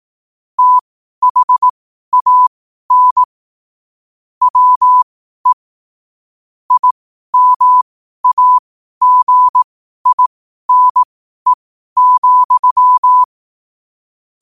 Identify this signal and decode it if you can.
Morse code, heard as a keyed tone, about 18 wpm, spaced as for 12 wpm, 1000 Hz. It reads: THAN WE IMAGINE,
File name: qotdfiles / 12wpm